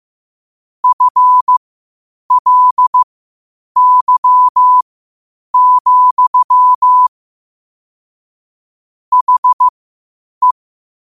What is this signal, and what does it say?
Morse code, heard as a keyed tone, about 15 wpm, spaced as for 8 wpm, 1000 Hz. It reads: FLY, HE
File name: qotdfiles / 8wpm